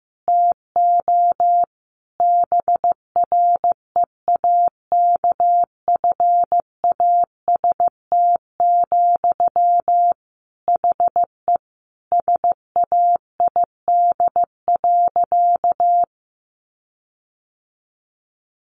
Morse code, heard as a keyed tone, about 15 wpm, 700 Hz. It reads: TO BREAKFAST, HE SAID.
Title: Sentences from Top 1500 Words 15wpm